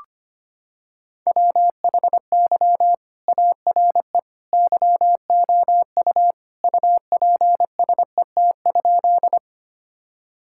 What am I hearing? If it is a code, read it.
Morse code, heard as a keyed tone, about 25 wpm, 700 Hz. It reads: WHY ARE YOU UPSET?
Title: Sentences from Top 1000 Words - Review Part 5 - 25wpm